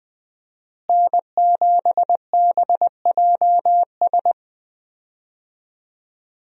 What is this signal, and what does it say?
Morse code, heard as a keyed tone, about 20 wpm, 700 Hz. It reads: N7BJS
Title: USA Call Signs - 5 Characters - Repeated 2x - 20wpm